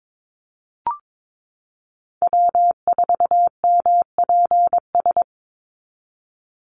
Morse code, heard as a keyed tone, about 22 wpm, 700 Hz. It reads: W4MPS
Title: NA SOTA Chaser Callsigns 22wpm